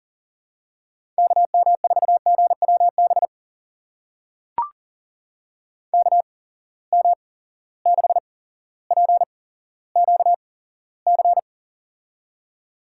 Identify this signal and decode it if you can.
Morse code, heard as a keyed tone, about 40 wpm, spaced as for 17 wpm, 700 Hz. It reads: KM4GWB KM6PQC
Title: USA Call Signs - 6 Characters - 8x Character Spacing - 40wpm